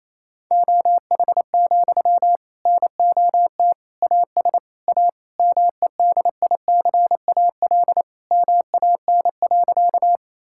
Morse code, heard as a keyed tone, about 28 wpm, 700 Hz. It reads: OH, NOT AS A MEDICAL MAN.
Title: Sentences from Top 1000 Words - Review Part 5 - 28wpm